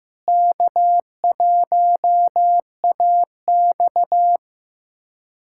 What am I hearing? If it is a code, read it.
Morse code, heard as a keyed tone, about 15 wpm, 700 Hz. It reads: K1AX
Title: USA Callsign DE Callsign - 4 Characters - 15wpm